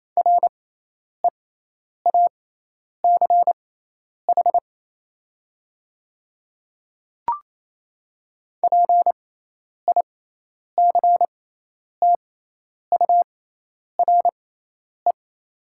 Morse code, heard as a keyed tone, about 28 wpm, spaced as for 10 wpm, 700 Hz. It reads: REACH PICTURE
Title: 6x Character Spacing - Top 500 Words - 28wpm